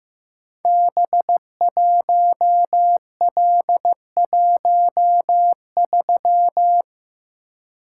Morse code, heard as a keyed tone, about 15 wpm, 700 Hz. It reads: B1L13
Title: Code Groups - 5 Letter:Numbers - 15wpm